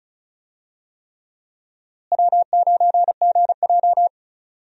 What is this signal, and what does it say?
Morse code, heard as a keyed tone, about 36 wpm, 700 Hz. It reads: W9GJ
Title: USA Call Signs - 4 Characters - Repeated 3x - 35wpm